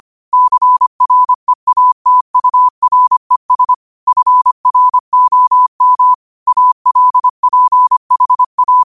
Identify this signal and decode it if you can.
Morse code, heard as a keyed tone, about 25 wpm, 1000 Hz. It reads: CREATURES FROM ALPHA
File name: qotdfiles / 25wpm